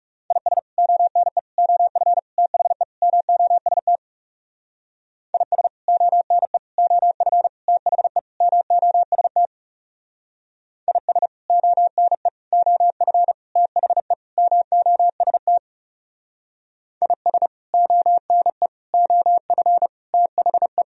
Morse code, heard as a keyed tone, about 36 wpm, 700 Hz. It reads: IS ONE OF THE MOST IS ONE OF THE MOST IS ONE OF THE MOST IS ONE OF THE
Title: Speed-Racing - Sets of 5 Words 30wpm